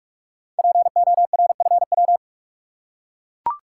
Morse code, heard as a keyed tone, about 44 wpm, 700 Hz. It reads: PORFW
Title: Code Groups - 5 Letters - 45wpm